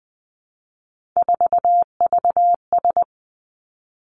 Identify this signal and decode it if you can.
Morse code, heard as a keyed tone, about 20 wpm, 700 Hz. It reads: 4VS
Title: Three Challenging Letter-Number Combinations 20wpm